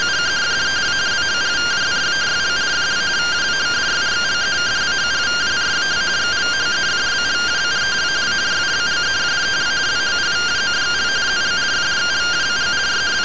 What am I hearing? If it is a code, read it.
Two-tone teleprinter signal, about 170 Hz shift.